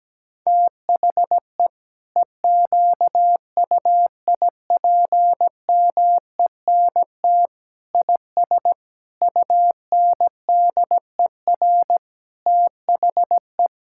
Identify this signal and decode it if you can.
Morse code, heard as a keyed tone, about 17 wpm, 700 Hz. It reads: THE EQUIPMENT IS UNDER THE
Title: Sentences from Top 1000 Words - Review Part 5 - 17wpm